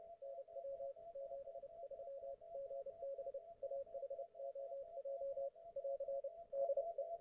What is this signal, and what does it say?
Morse code, heard as a keyed tone, about 30 wpm, 550 Hz. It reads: NW 73 GB AHOJ <AR> DE